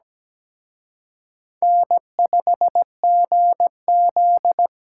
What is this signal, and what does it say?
Morse code, heard as a keyed tone, about 20 wpm, 700 Hz. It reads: N5GZ
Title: Speed Racing - USA 1x2 Call Signs 17wpm